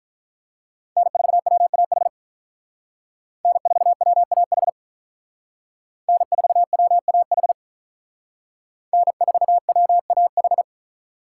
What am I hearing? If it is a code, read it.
Morse code, heard as a keyed tone, about 43 wpm, 700 Hz. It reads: N4WAH N4WAH N4WAH N4WAH
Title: Speed Racing - USA 1x3 Call Signs 35wpm